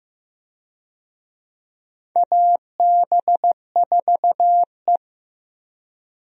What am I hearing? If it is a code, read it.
Morse code, heard as a keyed tone, about 15 wpm, 700 Hz. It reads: AB4E
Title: USA Call Signs - 4 Characters - Repeated 3x - 15wpm